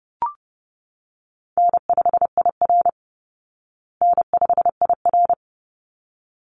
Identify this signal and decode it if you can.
Morse code, heard as a keyed tone, about 30 wpm, 700 Hz. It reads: N5IR N5IR
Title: Easy Call Signs Repeated 2x - 30wpm